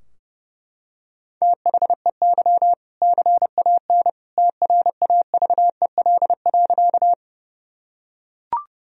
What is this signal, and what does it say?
Morse code, heard as a keyed tone, about 32 wpm, 700 Hz. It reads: THEY CAN TRAVEL.
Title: Sentences from Top 1000 Words - Review Part 2 - 30wpm